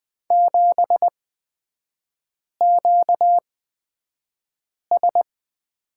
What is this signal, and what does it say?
Morse code, heard as a keyed tone, about 20 wpm, spaced as for 5 wpm, 700 Hz. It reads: 7QS
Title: 243 Lesson 39e - Letter z - Three Characters 20wpm